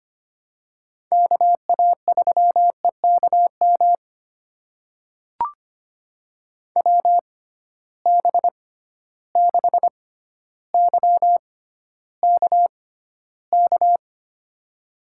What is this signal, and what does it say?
Morse code, heard as a keyed tone, about 25 wpm, spaced as for 13 wpm, 700 Hz. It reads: KA3EKM WB6YKK
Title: USA Call Signs - 6 Characters - 6x Character Spacing - 25wpm